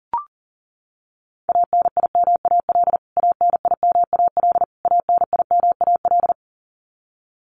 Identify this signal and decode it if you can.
Morse code, heard as a keyed tone, about 40 wpm, 700 Hz. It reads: ANIMAL ANIMAL ANIMAL
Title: Top 300 Words Repeated 40wpm